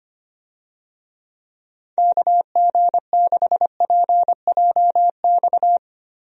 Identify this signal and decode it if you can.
Morse code, heard as a keyed tone, about 25 wpm, 700 Hz. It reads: KG6PJX
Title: USA Call Signs - 6 Characters - Repeated 2x - 25wpm